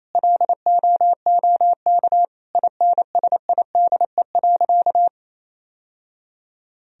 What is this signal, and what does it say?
Morse code, heard as a keyed tone, about 28 wpm, 700 Hz. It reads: LOOK INSIDE.
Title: Sentences from Top 1000 Words - Review Part 2 - 28wpm